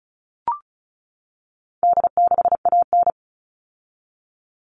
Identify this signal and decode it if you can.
Morse code, heard as a keyed tone, about 34 wpm, 700 Hz. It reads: D6AN
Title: All Cut Numbers - 4-Digit 35wpm